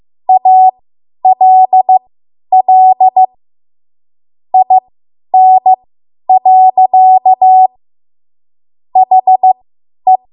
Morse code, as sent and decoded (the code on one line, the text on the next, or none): .- .-.. .-.. / .. -. .-.-.- / .... .
ALL IN. HE